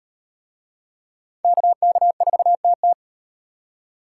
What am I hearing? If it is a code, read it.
Morse code, heard as a keyed tone, about 35 wpm, 700 Hz. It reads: KK4TT